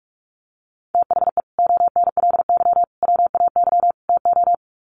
Morse code, heard as a keyed tone, about 45 wpm, 700 Hz. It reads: THE ONLY WAY TO
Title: Sets of 4 Words 45wpm